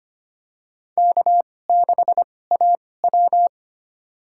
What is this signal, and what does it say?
Morse code, heard as a keyed tone, about 25 wpm, spaced as for 18 wpm, 700 Hz. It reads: K6AW
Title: USA Call Signs - 4 Characters - 2x Character Spacing - 25wpm